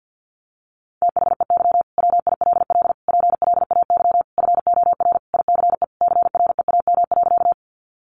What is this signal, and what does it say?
Morse code, heard as a keyed tone, about 50 wpm, 700 Hz. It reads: THEY WILL PLAY FOR ICE CREAM.
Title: Sentences from Top 1500 Words 50wpm